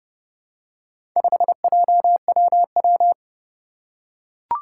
Morse code, heard as a keyed tone, about 30 wpm, 700 Hz. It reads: 5JWW
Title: Four Challenging Letter-Number Combinations 30wpm